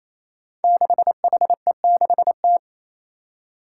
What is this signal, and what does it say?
Morse code, heard as a keyed tone, about 27 wpm, 700 Hz. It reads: 6HE6T
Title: Five Challenging Letter-Number Combinations 28wpm